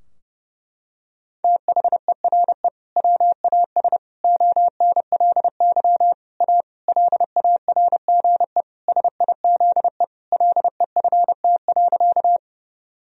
Morse code, heard as a keyed tone, about 30 wpm, 700 Hz. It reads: THERE WAS ONLY A LARGE SIZE LEFT.
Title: Sentences from Top 1000 Words - Review Part 2 - 30wpm